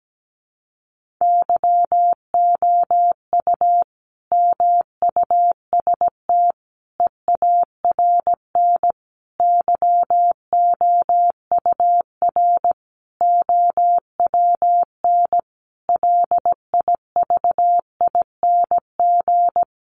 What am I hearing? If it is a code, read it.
Morse code, heard as a keyed tone, about 17 wpm, 700 Hz. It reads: YOU MUST EARN YOUR OWN LIVING
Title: Sentences from Top 1500 Words 17wpm